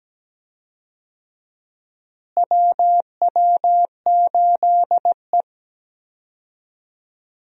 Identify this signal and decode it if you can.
Morse code, heard as a keyed tone, about 17 wpm, 700 Hz. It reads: WW8E
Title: USA Call Signs - 4 Characters - Repeated 3x - 17wpm